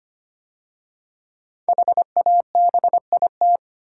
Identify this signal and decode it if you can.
Morse code, heard as a keyed tone, about 25 wpm, 700 Hz. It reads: HABIT
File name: 5 Letter Words - First 3 Letters Given - 25wpm